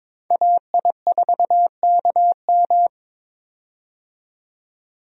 Morse code, heard as a keyed tone, about 22 wpm, 700 Hz. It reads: AI4KM
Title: USA Call Signs - 5 Characters - 22wpm